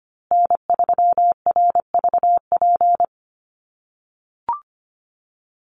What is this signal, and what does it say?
Morse code, heard as a keyed tone, about 25 wpm, 700 Hz. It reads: N3RVP
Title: USA Call Signs Repeated 2x 25wpm